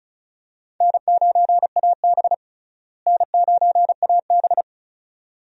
Morse code, heard as a keyed tone, about 36 wpm, 700 Hz. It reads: N9AB N9AB
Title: USA Call Signs - 4 Characters - Repeated 2x - 35wpm